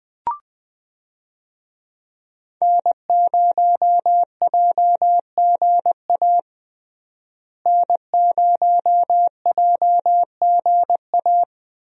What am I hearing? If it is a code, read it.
Morse code, heard as a keyed tone, about 20 wpm, 700 Hz. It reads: N0JGA N0JGA
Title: USA Call Signs - 5 Characters - Repeated 3x - 20wpm